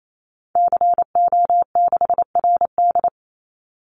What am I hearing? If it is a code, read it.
Morse code, heard as a keyed tone, about 28 wpm, 700 Hz. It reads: CO6RD
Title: Call Signs Repeated 2x - 28wpm